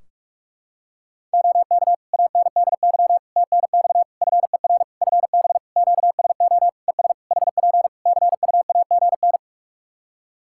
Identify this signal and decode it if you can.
Morse code, heard as a keyed tone, about 44 wpm, 700 Hz. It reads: OK ANDY TNX FER FB QSO ES HP CUAGN
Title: K1USN SST QSO Practice 45wpm